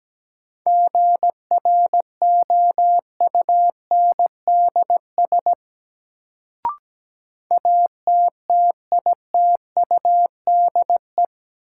Morse code, heard as a keyed tone, about 17 wpm, 700 Hz. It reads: GROUNDS ATTITUDE
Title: Sentences from Top 1300 Words - New Word Drill - 17wpm